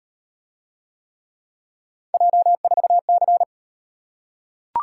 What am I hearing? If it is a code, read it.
Morse code, heard as a keyed tone, about 36 wpm, 700 Hz. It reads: J4C